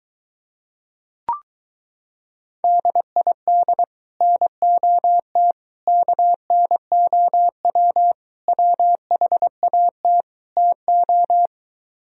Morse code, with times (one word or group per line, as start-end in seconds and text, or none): 2.6-3.8 DID
4.2-5.5 NOT
5.9-8.1 KNOW
8.5-10.2 WHAT
10.6-11.5 TO